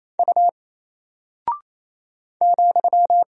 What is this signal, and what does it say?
Morse code, heard as a keyed tone, about 27 wpm, 700 Hz. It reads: U ,
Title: Single Character (Letter:Number:Punctuation:Prosign) 28wpm